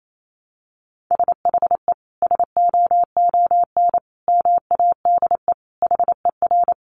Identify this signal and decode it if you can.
Morse code, heard as a keyed tone, about 28 wpm, 700 Hz. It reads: SHE SOON MADE HER
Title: Sentences from Top 1100 Words 28wpm